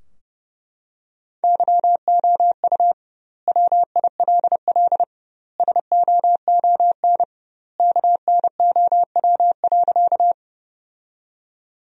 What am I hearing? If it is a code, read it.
Morse code, heard as a keyed tone, about 32 wpm, 700 Hz. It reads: YOU WILL SOON KNOW.
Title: 2x Word-Spacing - Sentences from Top 400 Words - 30wpm